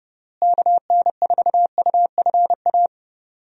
Morse code, heard as a keyed tone, about 30 wpm, 700 Hz. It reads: KN4UFA
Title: USA Call Signs - 6 Characters - Repeated 3x - 30wpm